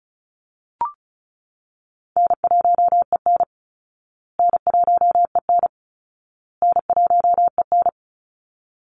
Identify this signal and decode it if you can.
Morse code, heard as a keyed tone, about 34 wpm, 700 Hz. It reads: N1EN N1EN N1EN
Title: Easy Call Signs Repeated 3x - 35wpm